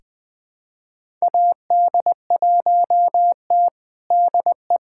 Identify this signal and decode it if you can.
Morse code, heard as a keyed tone, about 20 wpm, 700 Hz. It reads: AD1T DE
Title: USA Callsign DE Callsign - 4 Characters - 20wpm